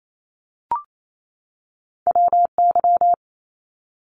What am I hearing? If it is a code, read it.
Morse code, heard as a keyed tone, about 28 wpm, 700 Hz. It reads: WY
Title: ARRL Sections 28wpm